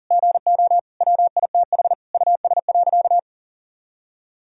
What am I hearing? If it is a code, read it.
Morse code, heard as a keyed tone, about 40 wpm, 700 Hz. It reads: GO WITH US.
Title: Sentences from Top 1000 Words - Review Part 3 - 40wpm